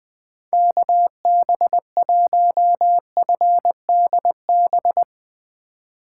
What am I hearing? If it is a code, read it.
Morse code, heard as a keyed tone, about 21 wpm, 700 Hz. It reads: KB1FDB
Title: USA Call Signs - 2x Character Spacing - 20wpm